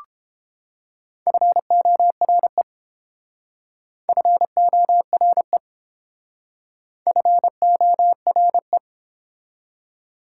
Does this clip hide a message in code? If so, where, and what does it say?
Yes: Morse code from 1.3 to 8.8 s, heard as a keyed tone, about 32 wpm, 700 Hz. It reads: FORE FORE FORE